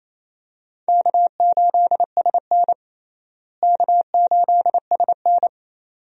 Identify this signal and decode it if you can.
Morse code, heard as a keyed tone, about 28 wpm, 700 Hz. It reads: K8SN K8SN
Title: USA Call Signs - 4 Characters - Repeated 2x - 28wpm